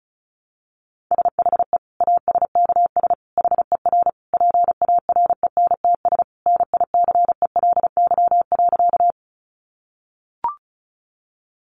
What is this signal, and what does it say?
Morse code, heard as a keyed tone, about 34 wpm, 700 Hz. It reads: SHE ASKS HER PARENTS NICELY.
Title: Easy Phrases 35wpm